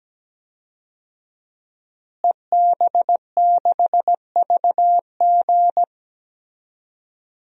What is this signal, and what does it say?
Morse code, heard as a keyed tone, about 17 wpm, 700 Hz. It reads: EB6VG
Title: DXCC-100 Call Signs - 5 Characters - Repeated 2x - 17wpm